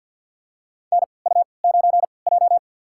Morse code, heard as a keyed tone, about 53 wpm, spaced as for 29 wpm, 700 Hz. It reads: NU9J
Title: USA Call Signs - 4 Characters - 3x Character Spacing - 50wpm